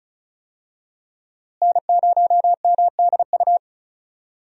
Morse code, heard as a keyed tone, about 36 wpm, 700 Hz. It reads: N0MDU
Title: USA Call Signs - 5 Characters - Repeated 3x - 35wpm